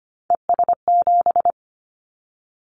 Morse code, heard as a keyed tone, about 25 wpm, 700 Hz. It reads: ES7
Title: Callsign Prefixes for Top 50 DXCC - 25wpm